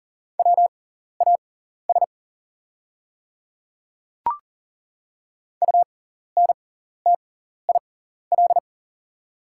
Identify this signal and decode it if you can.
Morse code, heard as a keyed tone, about 40 wpm, spaced as for 14 wpm, 690 Hz. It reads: WAS UNTIL
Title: 6x Character Spacing - Top 300 Words - 40wpm